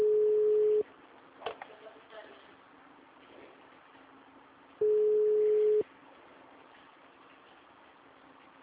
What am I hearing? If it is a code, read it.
Ringback tone (Europe, 425 Hz).
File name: out- -2011- - - .97